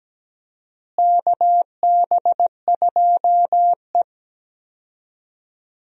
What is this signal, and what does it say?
Morse code, heard as a keyed tone, about 17 wpm, 700 Hz. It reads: KB2E
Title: USA Call Signs - 4 Characters - Repeated 3x - 17wpm